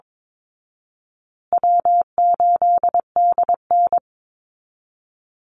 Morse code, heard as a keyed tone, about 22 wpm, 700 Hz. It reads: W8DN
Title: Easy Call Signs Repeated 2x - 22wpm